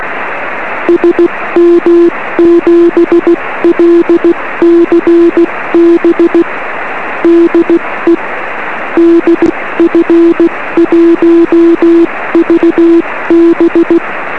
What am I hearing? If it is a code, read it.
Morse code, heard as a keyed tone, about 16 wpm, 340 Hz. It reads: SM7LCB DE DF1VB
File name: 1908z df1vb-p 13cm